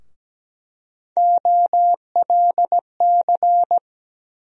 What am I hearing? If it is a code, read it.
Morse code, heard as a keyed tone, about 17 wpm, 700 Hz. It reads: OLC
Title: USA Call Sign Postfixes - 3 Letters 17wpm